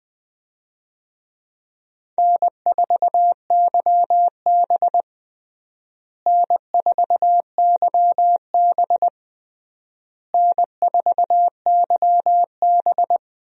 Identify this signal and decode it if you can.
Morse code, heard as a keyed tone, about 21 wpm, 700 Hz. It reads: N4YB N4YB N4YB
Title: USA Call Signs - 4 Characters - Repeated 3x - 20wpm